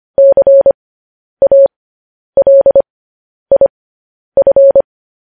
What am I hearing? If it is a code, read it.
Morse code, heard as a keyed tone, about 25 wpm, spaced as for 10 wpm, 550 Hz. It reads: CALIF
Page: news headlines in morse code at 10 wpm